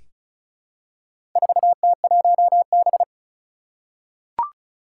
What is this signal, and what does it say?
Morse code, heard as a keyed tone, about 36 wpm, 700 Hz. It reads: 4T1B
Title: Four Challenging Letter-Number Combinations 35wpm